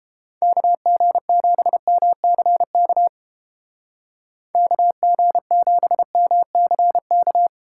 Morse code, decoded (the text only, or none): KG7MCK KG7MCK